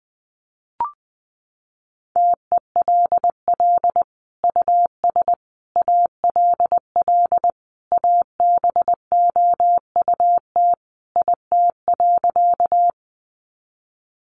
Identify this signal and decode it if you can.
Morse code, heard as a keyed tone, about 19 wpm, 700 Hz. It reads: TELL US ALL ABOUT IT.
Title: Sentences from Top 200 Words 20wpm